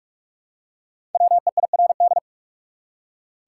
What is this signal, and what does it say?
Morse code, heard as a keyed tone, about 46 wpm, 700 Hz. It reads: WEIRD
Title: Five Letter Words - 45wpm